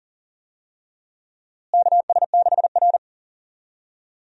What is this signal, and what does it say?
Morse code, heard as a keyed tone, about 40 wpm, 700 Hz. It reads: KS6R
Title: USA Call Signs - 4 Characters - Repeated 2x - 40wpm